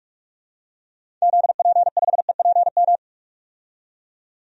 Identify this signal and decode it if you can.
Morse code, heard as a keyed tone, about 46 wpm, 700 Hz. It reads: ZP5EPM
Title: DXCC-100 Call Signs - 6 Characters - 45wpm